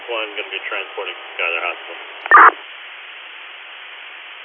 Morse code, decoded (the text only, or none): none